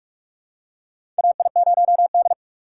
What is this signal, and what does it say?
Morse code, heard as a keyed tone, about 42 wpm, 700 Hz. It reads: AI0D
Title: USA Call Signs - 4 Characters - 2x Character Spacing - 45wpm